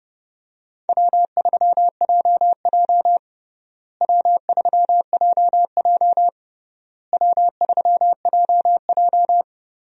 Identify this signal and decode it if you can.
Morse code, heard as a keyed tone, about 32 wpm, 700 Hz. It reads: W3JJ W3JJ W3JJ
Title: USA Call Signs - 4 Characters - Repeated 3x - 30wpm